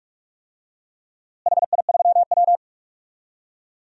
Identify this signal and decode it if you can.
Morse code, heard as a keyed tone, about 41 wpm, 700 Hz. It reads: HI3W
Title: DXCC-100 Call Signs - 4 Characters - Repeated 2x - 45wpm